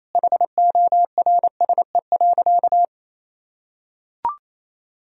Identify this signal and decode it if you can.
Morse code, heard as a keyed tone, about 28 wpm, 700 Hz. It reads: HORSE.
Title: Sentences from Top 1000 Words - Review Part 1 - 28wpm